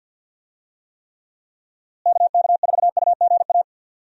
Morse code, heard as a keyed tone, about 53 wpm, 700 Hz. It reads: KK4UGA
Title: USA Call Signs - 6 Characters - Repeated 3x - 50wpm